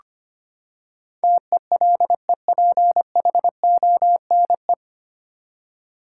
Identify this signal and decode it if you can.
Morse code, heard as a keyed tone, about 25 wpm, 700 Hz. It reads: TELEPHONE
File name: Nine Letter Words - 25wpm